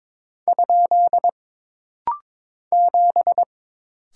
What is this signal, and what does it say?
Morse code, heard as a keyed tone, about 22 wpm, 700 Hz. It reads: ? 7
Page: This rapid-fire timing is geared to push your ICR (Instant Character Recognition) skills.